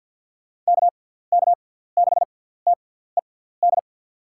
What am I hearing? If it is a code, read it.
Morse code, heard as a keyed tone, about 50 wpm, spaced as for 17 wpm, 700 Hz. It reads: KK6TED
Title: USA Call Signs - 6x Character Spacing - 50wpm